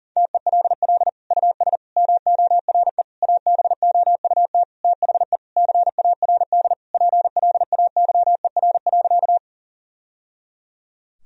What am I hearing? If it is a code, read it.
Morse code, heard as a keyed tone, about 40 wpm, 700 Hz. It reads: TELL US MORE ABOUT THE CARD PLAYER.
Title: Sentences from Top 1000 Words - Review Part 4 - 40wpm